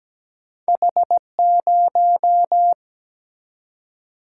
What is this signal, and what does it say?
Morse code, heard as a keyed tone, about 17 wpm, 700 Hz. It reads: H0
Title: Single-Letter Single-Number 17wpm